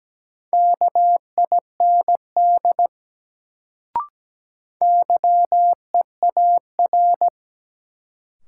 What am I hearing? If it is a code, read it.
Morse code, heard as a keyed tone, about 17 wpm, 700 Hz. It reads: KIND YEAR
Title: Sentences from Top 200 Words - New Word Drill - 17wpm